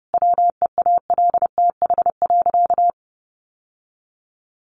Morse code, heard as a keyed tone, about 30 wpm, 700 Hz. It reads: WEALTH.
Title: Sentences from Top 1200 Words 30wpm